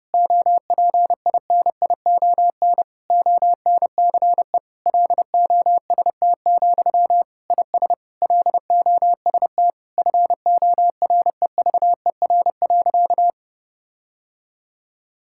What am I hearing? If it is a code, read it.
Morse code, heard as a keyed tone, about 32 wpm, 700 Hz. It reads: OPINION ONCE LOST, IS LOST FOREVER.
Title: Sentences from Top 1600 Words 30wpm